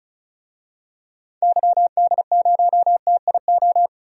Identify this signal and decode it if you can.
Morse code, heard as a keyed tone, about 36 wpm, 700 Hz. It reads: YD0TIO
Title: DXCC-100 Call Signs - 6 Characters - 35wpm